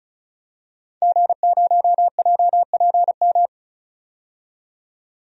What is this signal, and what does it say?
Morse code, heard as a keyed tone, about 36 wpm, 700 Hz. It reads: G0JPM
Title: DXCC-100 Call Signs - 5 Characters - 35wpm